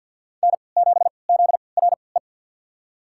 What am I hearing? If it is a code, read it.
Morse code, heard as a keyed tone, about 53 wpm, spaced as for 29 wpm, 690 Hz. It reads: N7ZRE
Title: USA Call Signs - 5 Characters - 3x Character Spacing - 50wpm